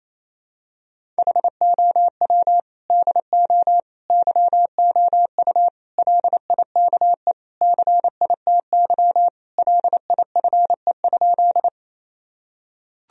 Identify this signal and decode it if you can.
Morse code, heard as a keyed tone, about 28 wpm, 700 Hz. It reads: HOW DO YOU LIKE CITY LIFE?
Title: Sentences from Top 1000 Words - Review Part 1 - 28wpm